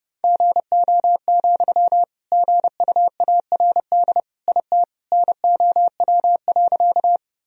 Morse code, heard as a keyed tone, about 30 wpm, 700 Hz. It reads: GO, GUARD IT NOW.
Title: Sentences from Top 1000 Words - Review Part 4 - 30wpm